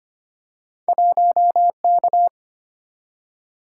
Morse code, heard as a keyed tone, about 25 wpm, 700 Hz. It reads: 1K
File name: Two Letter-Number Combinations - 25wpm